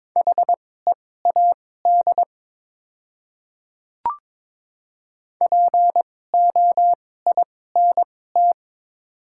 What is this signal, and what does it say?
Morse code, heard as a keyed tone, about 22 wpm, spaced as for 16 wpm, 700 Hz. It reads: HEAD POINT
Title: 2x Character Spacing - Top 300 Words - 22wpm